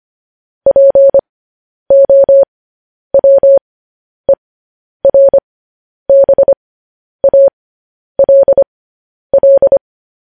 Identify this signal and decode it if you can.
Morse code, heard as a keyed tone, about 25 wpm, spaced as for 10 wpm, 550 Hz. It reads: POWERBALL